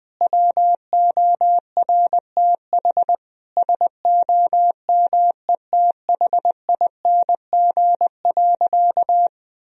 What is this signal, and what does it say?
Morse code, heard as a keyed tone, about 20 wpm, 700 Hz. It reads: WORTH SOMETHING.
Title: Sentences from Top 1000 Words - Review Part 1 - 20wpm